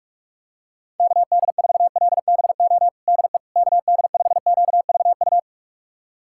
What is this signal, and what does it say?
Morse code, heard as a keyed tone, about 44 wpm, 690 Hz. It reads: KD4LBO DE KD5QVU
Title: USA Callsign DE Callsign - 6 Characters - 45wpm